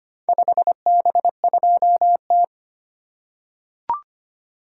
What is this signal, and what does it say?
Morse code, heard as a keyed tone, about 25 wpm, 700 Hz. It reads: 5B2T
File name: practice / 25wpm / Four Letter-Number Combinations - 25wpm